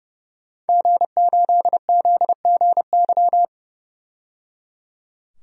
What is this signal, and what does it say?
Morse code, heard as a keyed tone, about 30 wpm, 700 Hz. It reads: G8ZGY
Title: DXCC-100 Call Signs - 5 Characters - 30wpm